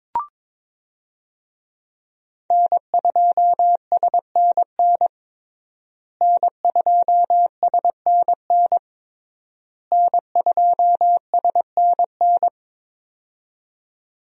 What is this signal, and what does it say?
Morse code, heard as a keyed tone, about 22 wpm, 700 Hz. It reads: N2SNN N2SNN N2SNN
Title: USA Call Signs - 5 Characters - Repeated 3x - 22wpm